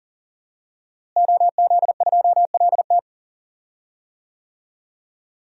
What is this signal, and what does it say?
Morse code, heard as a keyed tone, about 40 wpm, 700 Hz. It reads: OZ2LT
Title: DXCC-100 Call Signs - 5 Characters - Repeated 2x - 40wpm